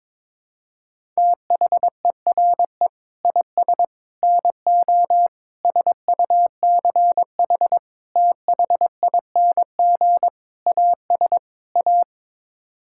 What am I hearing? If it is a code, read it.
Morse code, heard as a keyed tone, about 22 wpm, 700 Hz. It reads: THERE IS NO SUCH THING AS A
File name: practice / 22wpm / Sets of 7 Words - 22wpm